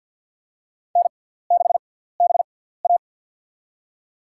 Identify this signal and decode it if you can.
Morse code, heard as a keyed tone, about 53 wpm, spaced as for 17 wpm, 700 Hz. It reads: N6BA
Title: USA Call Signs - 4 Characters - 6x Character Spacing - 50wpm